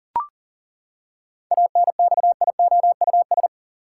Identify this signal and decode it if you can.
Morse code, heard as a keyed tone, about 40 wpm, 700 Hz. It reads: ANXIOUS
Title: Seven Letter Words - 40wpm